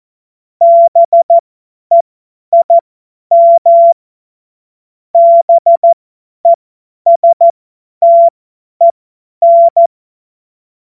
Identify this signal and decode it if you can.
Morse code, heard as a keyed tone, about 14 wpm, spaced as for 10 wpm, 680 Hz. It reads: BEIM BESTEN
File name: Morsekurs / Witze / Witz 17. Januar 2023 50 F70